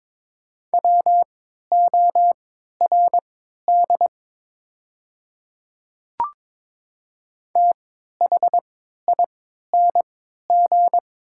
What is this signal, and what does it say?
Morse code, heard as a keyed tone, about 23 wpm, spaced as for 13 wpm, 700 Hz. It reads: WORD THING
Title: 3x Character Spacing - Top 100 Words - 22wpm